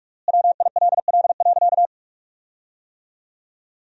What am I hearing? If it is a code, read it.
Morse code, heard as a keyed tone, about 44 wpm, 700 Hz. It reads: WILL.
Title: 3x Word-Spacing - Sentences from Top 500 Words - 45wpm